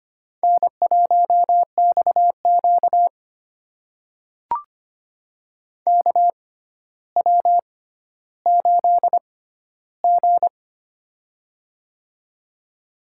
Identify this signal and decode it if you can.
Morse code, heard as a keyed tone, about 25 wpm, spaced as for 13 wpm, 700 Hz. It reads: N1XQ KW8G
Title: USA Call Signs - 4 Characters - 6x Character Spacing - 25wpm